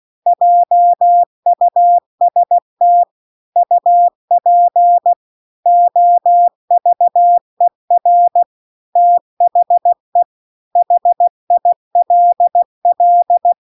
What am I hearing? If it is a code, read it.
Morse code, heard as a keyed tone, about 16 wpm, 700 Hz. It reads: JUST UP OVER THE HILL